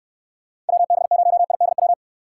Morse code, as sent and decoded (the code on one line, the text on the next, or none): .-- -... ----- .. -.. .--
WB0IDW